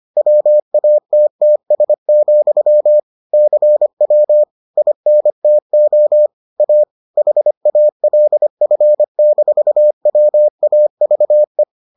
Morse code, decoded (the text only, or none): WATTS, CW INTO A HALF-WAVE